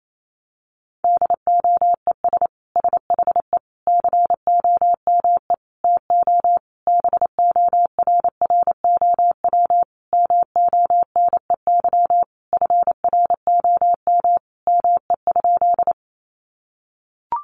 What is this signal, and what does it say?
Morse code, heard as a keyed tone, about 28 wpm, 700 Hz. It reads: DOES SHE COME TO BORROW MONEY FROM ME?
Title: Sentences from Top 1500 Words 28wpm